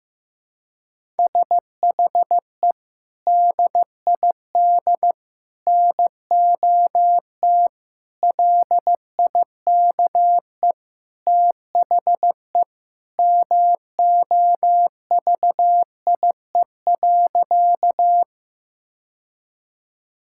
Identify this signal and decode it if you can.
Morse code, heard as a keyed tone, about 15 wpm, 700 Hz. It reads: SHE DID NOT LIKE THE MOVIE.
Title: Sentences from Top 1000 Words - Review Part 5 - 15wpm